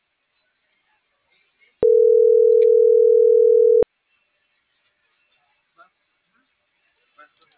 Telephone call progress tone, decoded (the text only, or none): ringback tone